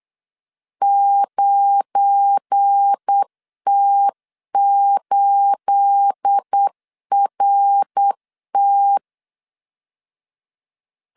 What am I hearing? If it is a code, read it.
Morse code, heard as a keyed tone, about 8 wpm, 800 Hz. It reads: 9T8RT
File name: LEZIONI / 9 28 8wpm 3512